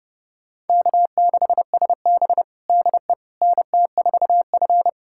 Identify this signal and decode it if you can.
Morse code, heard as a keyed tone, about 30 wpm, 700 Hz. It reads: K6SB DE NT4F
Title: USA Callsign DE Callsign - 4 Characters - 30wpm